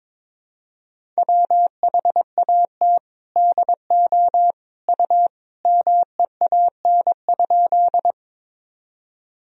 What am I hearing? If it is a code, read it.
Morse code, heard as a keyed tone, about 22 wpm, 700 Hz. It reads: WHAT DO U MEAN?
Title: Sentences from Top 200 Words and Abbreviations 22wpm